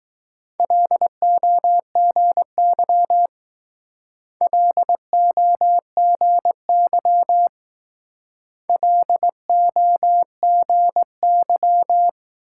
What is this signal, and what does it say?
Morse code, heard as a keyed tone, about 20 wpm, 700 Hz. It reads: LOGY LOGY LOGY